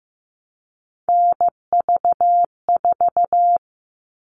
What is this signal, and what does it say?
Morse code, heard as a keyed tone, about 15 wpm, 700 Hz. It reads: NV4
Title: All Cut Numbers - 3-Digit 15wpm